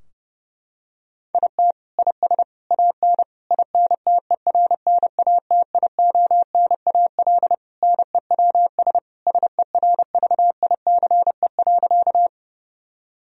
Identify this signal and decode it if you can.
Morse code, heard as a keyed tone, about 30 wpm, 700 Hz. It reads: IT IS AN INTERNATIONAL NEWS SERVICE.
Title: Sentences from Top 1000 Words - Review Part 4 - 30wpm